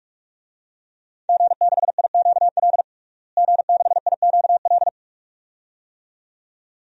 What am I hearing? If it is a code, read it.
Morse code, heard as a keyed tone, about 46 wpm, 700 Hz. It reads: G6IQL G6IQL
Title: DXCC-100 Call Signs - 5 Characters - Repeated 2x - 45wpm